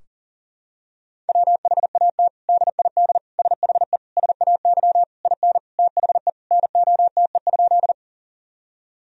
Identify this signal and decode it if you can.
Morse code, heard as a keyed tone, about 40 wpm, 700 Hz. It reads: WHAT DID SHE SAY IN THE NOTE?
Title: Sentences from Top 1000 Words - Review Part 2 - 40wpm